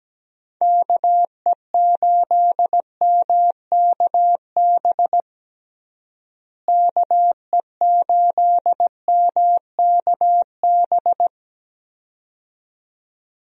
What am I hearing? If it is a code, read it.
Morse code, heard as a keyed tone, about 17 wpm, 700 Hz. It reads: KE8MKB KE8MKB
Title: USA Call Signs - 6 Characters - Repeated 2x - 17wpm